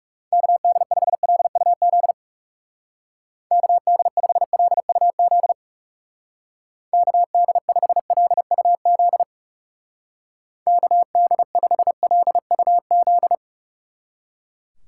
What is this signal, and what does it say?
Morse code, heard as a keyed tone, about 36 wpm, 700 Hz. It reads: KD5LUZ KD5LUZ KD5LUZ KD5LUZ
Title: Speed Racing - USA 2x3 Call Signs 30wpm